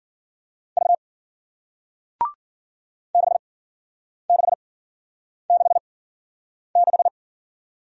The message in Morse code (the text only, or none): V 6666